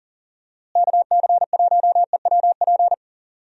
Morse code, heard as a keyed tone, about 40 wpm, 700 Hz. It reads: KC1EWP
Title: USA Call Signs - 6 Characters - Repeated 2x - 40wpm